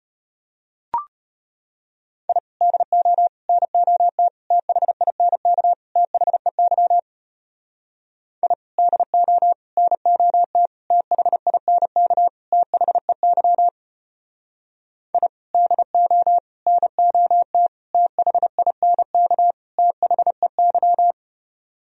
Morse code, heard as a keyed tone, about 33 wpm, 700 Hz. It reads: I DO NOT THINK THEY I DO NOT THINK THEY I DO NOT THINK THEY